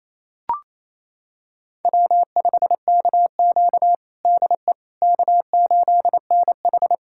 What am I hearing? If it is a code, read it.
Morse code, heard as a keyed tone, about 29 wpm, 690 Hz. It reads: W5KQ DE K8NH
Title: USA Callsign DE Callsign - 4 Characters - 28wpm